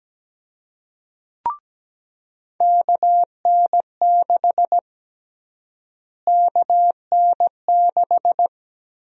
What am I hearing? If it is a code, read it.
Morse code, heard as a keyed tone, about 17 wpm, 700 Hz. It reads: KN6 KN6
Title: USA Call Sign Prefixes - Repeated 2x - 17wpm